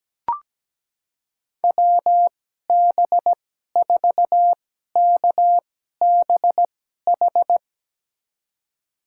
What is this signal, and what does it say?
Morse code, heard as a keyed tone, about 17 wpm, spaced as for 12 wpm, 700 Hz. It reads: WB4KBH
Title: USA Call Signs - 2x Character Spacing - 17wpm